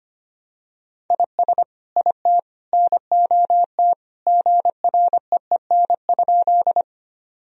Morse code, heard as a keyed tone, about 25 wpm, 700 Hz. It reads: IS IT NOT GREEN?
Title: Sentences from Top 1000 Words - Review Part 2 - 25wpm